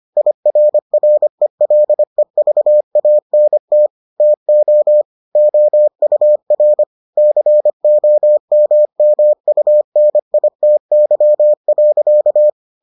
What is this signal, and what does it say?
Morse code, heard as a keyed tone, about 25 wpm, 600 Hz. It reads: IRRELEVANT TO OUR COMMUNITY.